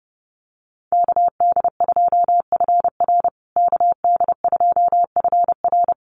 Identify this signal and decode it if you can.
Morse code, heard as a keyed tone, about 28 wpm, 700 Hz. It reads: KD2FR KD2FR
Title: USA Call Signs Repeated 2x 30wpm